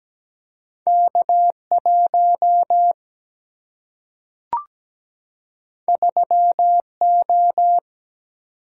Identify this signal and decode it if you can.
Morse code, heard as a keyed tone, about 17 wpm, 700 Hz. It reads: K1 3O
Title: Single-Letter Single-Number 17wpm